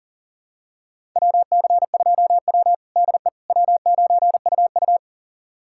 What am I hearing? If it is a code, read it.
Morse code, heard as a keyed tone, about 40 wpm, 700 Hz. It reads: WC2W DE W9UU
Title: USA Callsign DE Callsign - 4 Characters - 40wpm